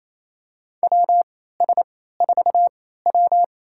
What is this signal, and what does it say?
Morse code, heard as a keyed tone, about 28 wpm, spaced as for 16 wpm, 700 Hz. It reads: WS4W
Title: USA Call Signs - 3x Character Spacing - 28wpm